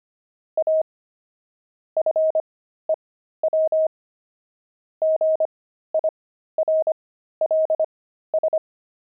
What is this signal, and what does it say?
Morse code, heard as a keyed tone, about 25 wpm, spaced as for 13 wpm, 630 Hz. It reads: A FEW GIRLS
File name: pangram-fs-13-cs-25-antique